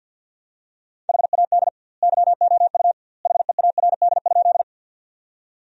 Morse code, heard as a keyed tone, about 52 wpm, 700 Hz. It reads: HAD YOU HEARD?
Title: 2x Word-Spacing - Sentences from Top 400 Words - 50wpm